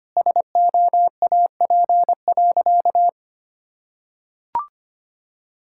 Morse code, heard as a keyed tone, about 25 wpm, 700 Hz. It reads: SOAP.
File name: Easy Sentences with Hints - 25wpm